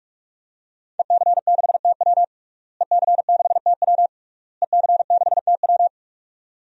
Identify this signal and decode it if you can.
Morse code, heard as a keyed tone, about 46 wpm, 700 Hz. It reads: EC6TW EC6TW EC6TW
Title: DXCC-100 Call Signs - 5 Characters - Repeated 3x - 45wpm